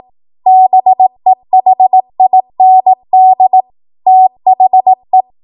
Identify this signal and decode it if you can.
Morse code, heard as a keyed tone, about 18 wpm, 750 Hz. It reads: BEHIND THE
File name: Morse / 18WPM